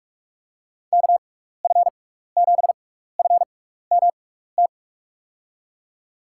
Morse code, heard as a keyed tone, about 46 wpm, spaced as for 15 wpm, 700 Hz. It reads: KF7FMT
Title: USA Call Signs - 6 Characters - 6x Character Spacing - 45wpm